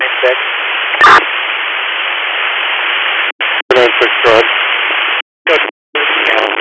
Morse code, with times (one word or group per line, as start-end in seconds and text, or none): none